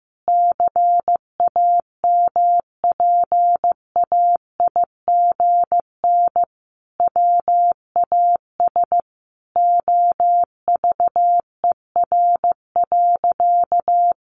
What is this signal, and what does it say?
Morse code, heard as a keyed tone, about 15 wpm, 700 Hz. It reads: CAMPAIGN WAS OVER.
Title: Sentences from Top 1200 Words 15wpm